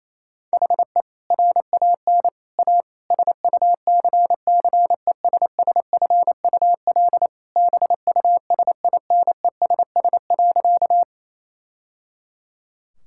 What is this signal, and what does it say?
Morse code, heard as a keyed tone, about 27 wpm, 700 Hz. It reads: HE RAN A SUCCESSFUL BUSINESS.
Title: Sentences from Top 1000 Words - Review Part 6 - 28wpm